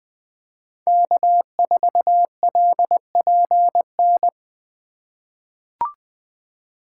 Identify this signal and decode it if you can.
Morse code, heard as a keyed tone, about 20 wpm, 700 Hz. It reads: K4LPN
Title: USA Call Signs - 2x Character Spacing - 20wpm